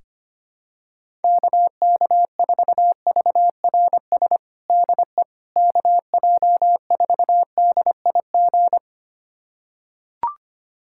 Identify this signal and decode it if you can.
Morse code, heard as a keyed tone, about 25 wpm, 700 Hz. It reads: KK4VRS DE KJ4DIG
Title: USA Callsign DE Callsign - 6 Characters - 25wpm